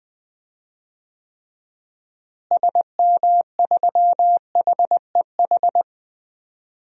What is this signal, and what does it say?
Morse code, heard as a keyed tone, about 20 wpm, 700 Hz. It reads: SM3HEH
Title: DXCC-100 Call Signs - 6 Characters - Repeated 2x - 20wpm